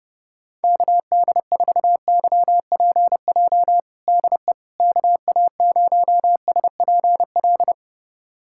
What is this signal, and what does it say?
Morse code, heard as a keyed tone, about 30 wpm, 700 Hz. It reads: KD4YPJ DE KA0SPL
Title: USA Callsign DE Callsign - 6 Characters - 30wpm